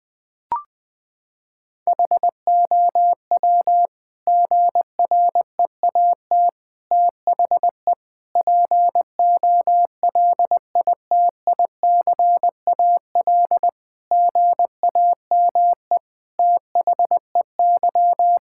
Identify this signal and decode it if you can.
Morse code, heard as a keyed tone, about 20 wpm, 700 Hz. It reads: HOW GREAT THE POLITICAL GAME THEY
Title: Sentences from Top 1000 Words - Review Part 5 - 20wpm